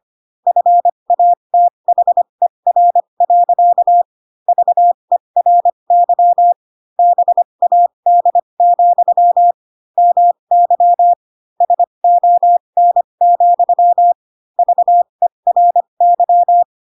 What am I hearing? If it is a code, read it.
Morse code, heard as a keyed tone, about 25 wpm, spaced as for 22 wpm, 700 Hz. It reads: FATHER. VERY BAD, MY SON, VERY
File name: Pinocchio / Pinocchio ch3 18 25 ik1hge